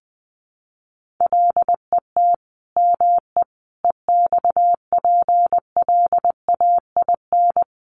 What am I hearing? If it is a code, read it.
Morse code, heard as a keyed tone, about 20 wpm, 700 Hz. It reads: LET ME EXPLAIN
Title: Sentences from Top 1300 Words 20wpm